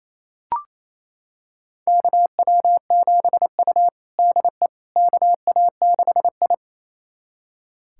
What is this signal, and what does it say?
Morse code, heard as a keyed tone, about 28 wpm, 700 Hz. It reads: KW7U DE KA6I
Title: USA Callsign DE Callsign - 4 Characters - 28wpm